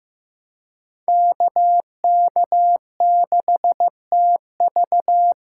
Morse code, heard as a keyed tone, about 15 wpm, 700 Hz. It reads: KK6TV
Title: K1USN SST QSO Practice - 15wpm